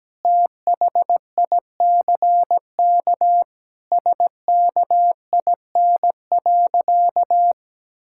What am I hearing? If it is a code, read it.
Morse code, heard as a keyed tone, about 17 wpm, 700 Hz. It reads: THICK SKIN.
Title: Sentences from Top 1600 Words 17wpm